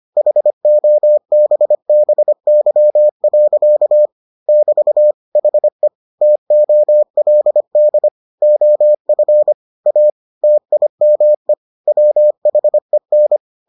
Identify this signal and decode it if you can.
Morse code, heard as a keyed tone, about 25 wpm, 600 Hz. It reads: HOBBY. <BT> HE TOLD OF A TIME WHEN